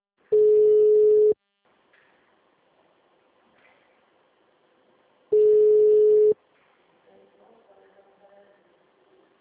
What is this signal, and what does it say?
Ringback tone (Europe, 425 Hz), with speech.
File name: out- -2008- - - .12288